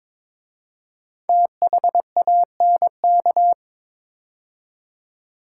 Morse code, heard as a keyed tone, about 22 wpm, 700 Hz. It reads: THANK
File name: 5 Letter Words - First 3 Letters Given - 22wpm